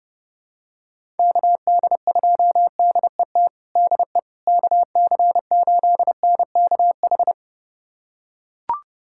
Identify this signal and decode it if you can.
Morse code, heard as a keyed tone, about 29 wpm, 700 Hz. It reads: KD2DET DE KC8NKH
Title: USA Callsign DE Callsign - 6 Characters - 30wpm